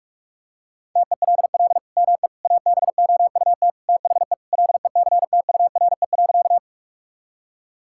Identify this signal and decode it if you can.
Morse code, heard as a keyed tone, about 46 wpm, 700 Hz. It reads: TELL ME ABOUT THE LECTURE.
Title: Sentences from Top 1600 Words 45wpm